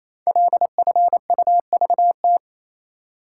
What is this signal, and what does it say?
Morse code, heard as a keyed tone, about 27 wpm, 700 Hz. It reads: LFUVT
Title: Code Groups - 5 Letters - 28wpm